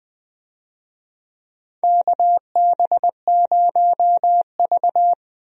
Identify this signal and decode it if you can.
Morse code, heard as a keyed tone, about 20 wpm, 700 Hz. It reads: KB0V
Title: USA Call Signs - 4 Characters - Repeated 3x - 20wpm